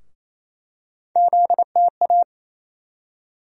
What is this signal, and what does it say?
Morse code, heard as a keyed tone, about 28 wpm, 700 Hz. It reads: ZTA
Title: USA Call Sign Postfixes - 3 Letters 28wpm